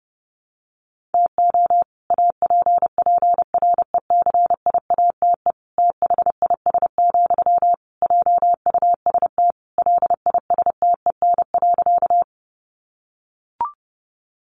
Morse code, heard as a keyed tone, about 28 wpm, 700 Hz. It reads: TO APPRECIATE THIS, JUST LISTEN.
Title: Sentences from Top 1300 Words 30wpm